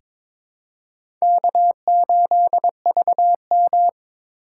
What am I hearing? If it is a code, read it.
Morse code, heard as a keyed tone, about 22 wpm, 700 Hz. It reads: K8VM
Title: USA Call Signs - 4 Characters - 4x Character Spacing - 22wpm